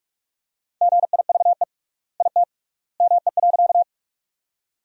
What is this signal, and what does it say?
Morse code, heard as a keyed tone, about 46 wpm, 700 Hz. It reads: GIVE IT ME.
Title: 3x Word-Spacing - Sentences from Top 200 Words - 45wpm